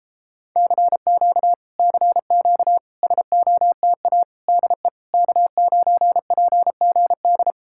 Morse code, heard as a keyed tone, about 38 wpm, 700 Hz. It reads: CQ CQ SOTA DE K9PGD